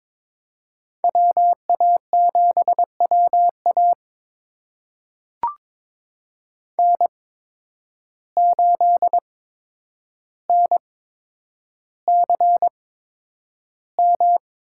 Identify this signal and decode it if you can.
Morse code, heard as a keyed tone, about 21 wpm, spaced as for 9 wpm, 700 Hz. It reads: WA7WA N8NCM
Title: USA Call Signs - 5 Characters - 8x Character Spacing - 22wpm